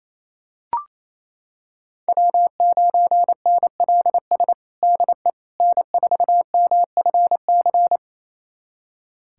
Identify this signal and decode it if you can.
Morse code, heard as a keyed tone, about 28 wpm, 700 Hz. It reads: W9NLS DE N4MFC
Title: USA Callsign DE Callsign 28wpm